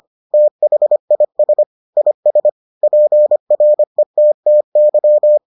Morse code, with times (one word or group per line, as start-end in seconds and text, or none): 0.3-1.6 THIS
2.0-2.5 IS
2.8-5.4 PRETTY